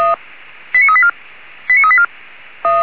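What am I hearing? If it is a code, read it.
Five-tone selective call (ZVEI), heard as quick runs of five tones, tones of about 70 ms. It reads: 89384 89384
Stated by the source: ZVEI five-tone call received by SDR.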